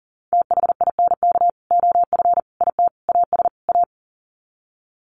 Morse code, heard as a keyed tone, about 40 wpm, 700 Hz. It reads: THINK OF IT AS A
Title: Sets of 5 Words Encore2 40wpm